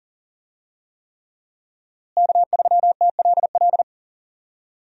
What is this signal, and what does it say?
Morse code, heard as a keyed tone, about 40 wpm, 700 Hz. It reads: K3TLL
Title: USA Call Signs - 5 Characters - Repeated 3x - 40wpm